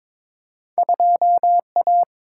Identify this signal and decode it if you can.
Morse code, heard as a keyed tone, about 22 wpm, 700 Hz. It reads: 2A
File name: practice / 22wpm / Two Letter-Number Combinations - 22wpm